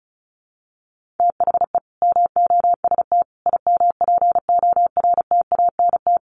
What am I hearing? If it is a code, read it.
Morse code, heard as a keyed tone, about 34 wpm, 700 Hz. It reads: THE MOST IMPORTANT
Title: Sets of 4 Words 35wpm